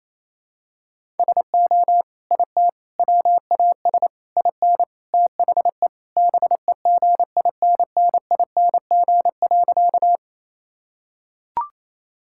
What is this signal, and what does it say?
Morse code, heard as a keyed tone, about 28 wpm, 700 Hz. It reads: SO IT WAS IN THE BEGINNING.
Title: Sentences from Top 1000 Words - Review Part 5 - 28wpm